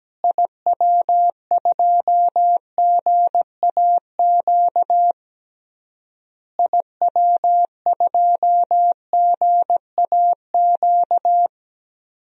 Morse code, heard as a keyed tone, about 17 wpm, 700 Hz. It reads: IW2GAQ IW2GAQ
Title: DXCC-100 Call Signs - 6 Characters - Repeated 3x - 17wpm